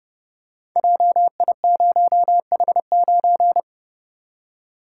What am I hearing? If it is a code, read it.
Morse code, heard as a keyed tone, about 30 wpm, 700 Hz. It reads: JI0H9
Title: Code Groups - 5 Letter:Numbers - 30wpm